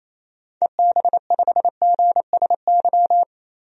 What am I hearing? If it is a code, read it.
Morse code, heard as a keyed tone, about 28 wpm, 700 Hz. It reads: EB5GSY
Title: DXCC-100 Call Signs - 6 Characters - 28wpm